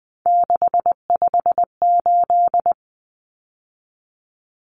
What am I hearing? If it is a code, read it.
Morse code, heard as a keyed tone, about 20 wpm, 700 Hz. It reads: 658
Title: Numbers - 3-Digit 20wpm